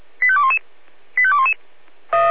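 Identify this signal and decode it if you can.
Five-tone selective call (ZVEI), heard as quick runs of five tones, tones of about 70 ms. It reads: 85211 85211
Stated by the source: ZVEI five-tone selective call received on an SDR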